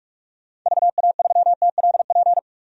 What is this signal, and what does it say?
Morse code, heard as a keyed tone, about 47 wpm, 700 Hz. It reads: VA3TLP
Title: DXCC-100 Call Signs - 6 Characters - 45wpm